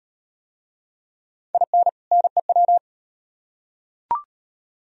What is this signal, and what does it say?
Morse code, heard as a keyed tone, about 36 wpm, 700 Hz. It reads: IN NEW